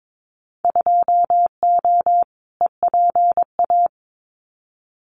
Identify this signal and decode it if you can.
Morse code, heard as a keyed tone, about 22 wpm, 700 Hz. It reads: 2O EPA
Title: Winter Field Day Exchanges Encore 22wpm